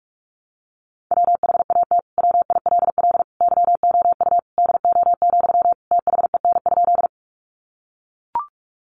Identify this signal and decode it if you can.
Morse code, heard as a keyed tone, about 44 wpm, 700 Hz. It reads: WHAT WILL YOU DO, THEN?
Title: Sentences from Top 100 Words 45wpm